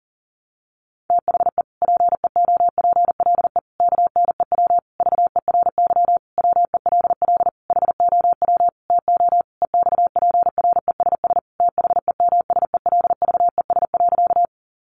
Morse code, heard as a keyed tone, about 40 wpm, 700 Hz. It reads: THE PEOPLE KNEW VERY WELL HOW TO EXPRESS THEMSELVES.
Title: Sentences from Top 1100 Words 40wpm